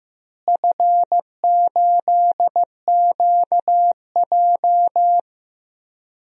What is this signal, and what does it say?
Morse code, heard as a keyed tone, about 15 wpm, 700 Hz. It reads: F8QJ
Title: DXCC-100 Call Signs - 4 Characters - Repeated 3x - 15wpm